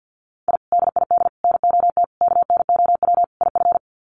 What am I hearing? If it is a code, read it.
Morse code, heard as a keyed tone, about 50 wpm, 700 Hz. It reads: I DID NOT KNOW IF
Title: Sets of 5 Words Encore 50wpm